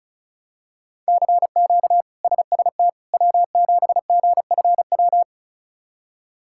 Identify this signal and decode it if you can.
Morse code, heard as a keyed tone, about 34 wpm, 700 Hz. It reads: CQ SST W7GFW
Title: K1USN SST QSO Practice - 35wpm